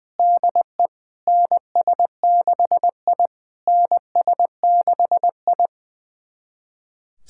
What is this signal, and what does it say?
Morse code, heard as a keyed tone, about 20 wpm, 690 Hz. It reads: DE NS6I NS6I
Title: SOTA QSO Practice 20wpm